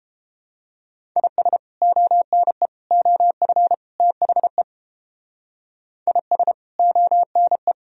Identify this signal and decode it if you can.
Morse code, heard as a keyed tone, about 30 wpm, 700 Hz. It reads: IS ONE OF THE IS ONE